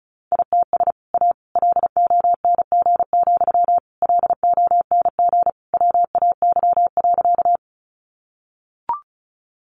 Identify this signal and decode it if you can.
Morse code, heard as a keyed tone, about 34 wpm, 700 Hz. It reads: ITS A LONG, LONG WAY.
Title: Sentences from Top 600 Words 35wpm